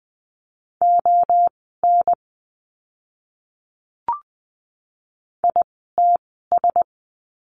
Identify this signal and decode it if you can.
Morse code, heard as a keyed tone, about 20 wpm, spaced as for 15 wpm, 700 Hz. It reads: ON ITS
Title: 237 Lesson 38f - Letter q - Words 20wpm